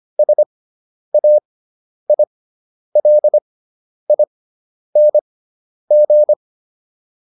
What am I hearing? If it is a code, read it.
Morse code, heard as a keyed tone, about 25 wpm, spaced as for 10 wpm, 600 Hz. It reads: SAILING